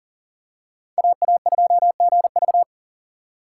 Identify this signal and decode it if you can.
Morse code, heard as a keyed tone, about 40 wpm, 700 Hz. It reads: AA2GV
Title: USA Call Signs - 5 Characters - 4x Character Spacing - 40wpm